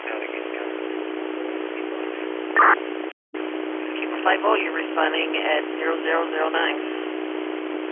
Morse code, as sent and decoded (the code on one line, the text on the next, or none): none